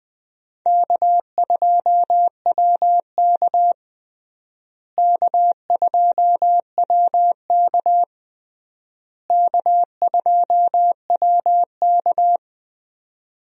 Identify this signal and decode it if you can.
Morse code, heard as a keyed tone, about 20 wpm, 700 Hz. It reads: K2WK K2WK K2WK
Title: USA Call Signs - 4 Characters - Repeated 3x - 20wpm